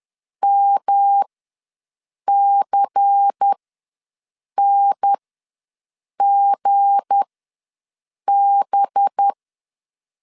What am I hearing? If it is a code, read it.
Morse code, heard as a keyed tone, about 10 wpm, spaced as for 6 wpm, 800 Hz. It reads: MCNGB
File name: LEZIONI / 7 GF9Z / 7 G ws 10wpm 3843